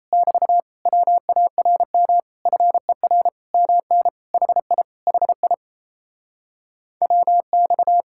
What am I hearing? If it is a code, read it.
Morse code, heard as a keyed tone, about 28 wpm, 700 Hz. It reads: <BT> WARM FER MN HI HI WX
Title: Speed-Racing Common QSO Phrases Encore - 28wpm